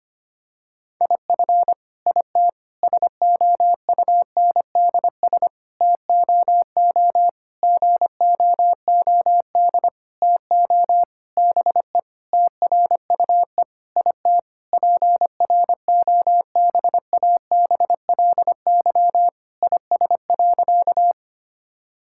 Morse code, heard as a keyed tone, about 25 wpm, 700 Hz. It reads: IF IT SOUNDS TOO GOOD TO BE TRUE IT PROBABLY IS.
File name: practice / 25wpm / Sentences with 9-Word Phrases - 25wpm